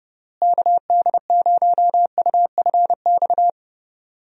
Morse code, heard as a keyed tone, about 30 wpm, 700 Hz. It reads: KD0UFX
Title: USA Call Signs - 6 Characters - Repeated 2x - 30wpm